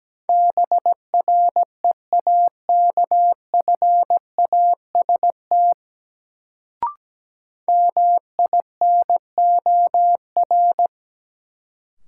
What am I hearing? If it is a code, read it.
Morse code, heard as a keyed tone, about 17 wpm, 700 Hz. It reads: BREAKFAST MINOR
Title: Sentences from Top 1500 Words - New Word Drill - 17wpm